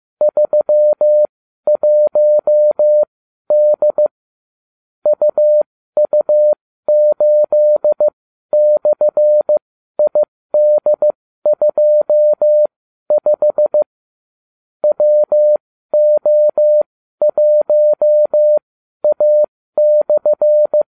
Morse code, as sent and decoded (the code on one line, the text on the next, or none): ...-- .---- -.. / ..- ..- ---.. -..-. .. -.. ..--- ..... / .-- --- .---- .- -..-.
31D UU8/ID25 WO1A/